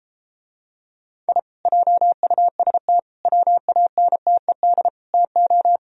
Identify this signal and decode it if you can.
Morse code, heard as a keyed tone, about 33 wpm, 700 Hz. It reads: I JUST WANTED TO